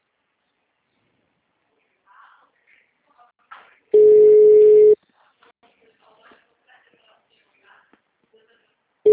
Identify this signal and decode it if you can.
Ringback tone (Europe, 425 Hz), with speech.
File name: out- -2001- - - .12809